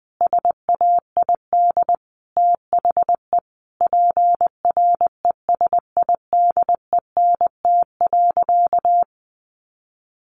Morse code, heard as a keyed tone, about 20 wpm, 700 Hz. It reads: SAID THE PRESIDENT.
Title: Sentences from Top 1000 Words 20wpm